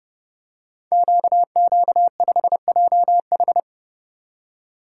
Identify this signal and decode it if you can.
Morse code, heard as a keyed tone, about 30 wpm, 700 Hz. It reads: QQ5JH
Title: Code Groups - 5 Letter:Numbers - 30wpm